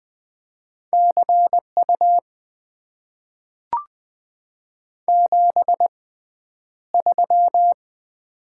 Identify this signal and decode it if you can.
Morse code, heard as a keyed tone, about 20 wpm, spaced as for 10 wpm, 700 Hz. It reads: CU 73
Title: Top 90 QSO Elements - 6x Character Spacing - 20wpm